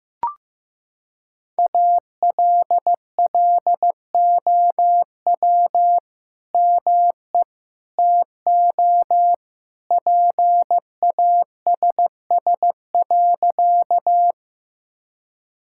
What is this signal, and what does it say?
Morse code, heard as a keyed tone, about 15 wpm, 700 Hz. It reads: ALLOW ME TO PASS.
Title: Sentences from Top 1000 Words - Review Part 5 - 15wpm